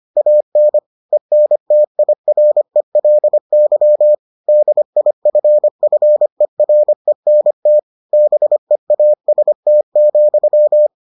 Morse code, heard as a keyed tone, about 25 wpm, 600 Hz. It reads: AN ENTIRELY DIFFERENT BEAST,